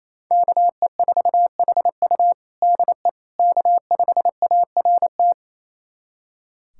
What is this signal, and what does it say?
Morse code, heard as a keyed tone, about 28 wpm, 700 Hz. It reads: KE4HU DE K5ART
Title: USA Callsign DE Callsign - 5 Characters - 28wpm